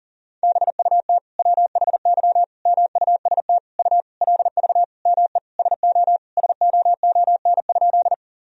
Morse code, heard as a keyed tone, about 40 wpm, 700 Hz. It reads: BUT WHY MUST U LV ME SO SOON?
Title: Sentences from Top 400 Words and Abbreviations 40wpm